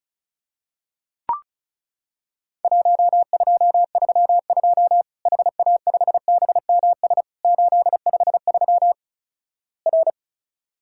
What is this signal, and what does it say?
Morse code, heard as a keyed tone, about 36 wpm, 700 Hz. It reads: 1232 HA5BMS 853 R
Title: Worked All Europe DX Contest - QTC Exchange - 35wpm